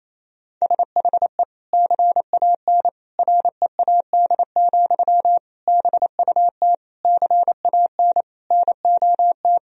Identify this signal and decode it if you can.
Morse code, heard as a keyed tone, about 28 wpm, 700 Hz. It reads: SHE CAN READ, BUT CAN NOT
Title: Sentences from Top 1000 Words - Review Part 1 - 28wpm